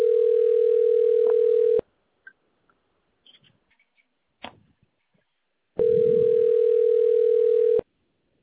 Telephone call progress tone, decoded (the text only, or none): ringback tone